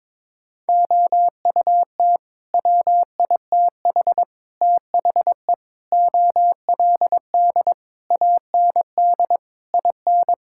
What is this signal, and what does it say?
Morse code, heard as a keyed tone, about 22 wpm, 700 Hz. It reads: OUT WITH THE OLD AND IN
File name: Sets of 9 Words - 22wpm